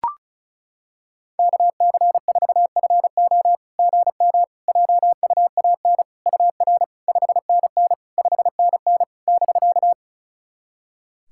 Morse code, heard as a keyed tone, about 36 wpm, 700 Hz. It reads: KC4FO GM JUAN UR 5NN 5NN <BK>
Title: SOTA QSO Practice 35wpm